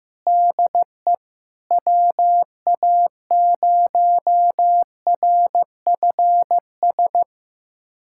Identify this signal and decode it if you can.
Morse code, heard as a keyed tone, about 16 wpm, 700 Hz. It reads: DE WA0RFS
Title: USA Callsign DE Callsign - 6 Characters - 15wpm